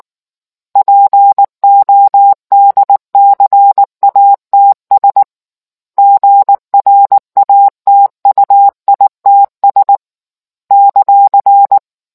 Morse code, heard as a keyed tone, about 19 wpm, 800 Hz. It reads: PODCATS GRATUITS ;